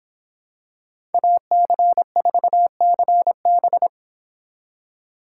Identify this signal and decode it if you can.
Morse code, heard as a keyed tone, about 26 wpm, 700 Hz. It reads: AC4CB